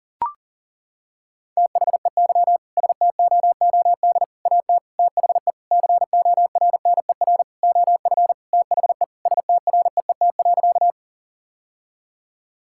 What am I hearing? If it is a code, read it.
Morse code, heard as a keyed tone, about 40 wpm, 700 Hz. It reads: THEY STOOD AT THE CORNER OF THE STREET.
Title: Sentences from Top 1600 Words 40wpm